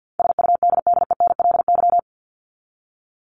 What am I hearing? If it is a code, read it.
Morse code, heard as a keyed tone, about 50 wpm, 700 Hz. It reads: SUDDENLY
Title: Top 1501-1750 Words 50wpm